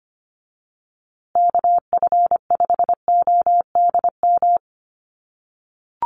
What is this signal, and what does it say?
Morse code, heard as a keyed tone, about 25 wpm, 700 Hz. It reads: KF5ODM
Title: USA Call Signs Repeated 3x 25wpm